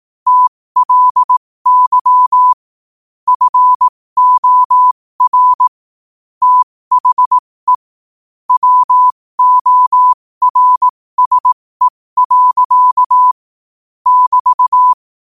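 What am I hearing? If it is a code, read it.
Morse code, heard as a keyed tone, about 18 wpm, spaced as for 15 wpm, 1000 Hz. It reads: TLY FOR THE WORSE. <BT>
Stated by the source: Morse code quote of the day sent at 15 WPM.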